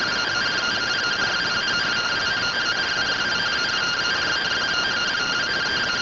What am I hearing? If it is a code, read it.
Two-tone teleprinter signal, about 170 Hz shift.
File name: findings / AXIAL fr45 3mm.gen-04 rtty rx